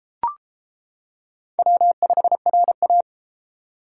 Morse code, heard as a keyed tone, about 33 wpm, 700 Hz. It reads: W5RA